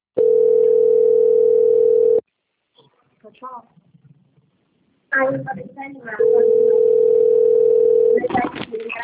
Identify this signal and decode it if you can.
Ringback tone (North America), 440 + 480 Hz, with speech.
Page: Capital - SP